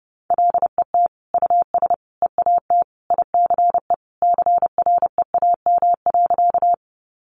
Morse code, heard as a keyed tone, about 30 wpm, 700 Hz. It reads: LET US EAT ICE CREAM.
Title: Sentences from Top 1500 Words 30wpm